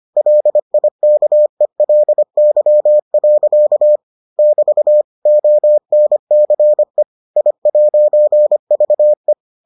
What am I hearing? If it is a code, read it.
Morse code, heard as a keyed tone, about 25 wpm, 600 Hz. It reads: LIKELY. <BT> ONCE I'VE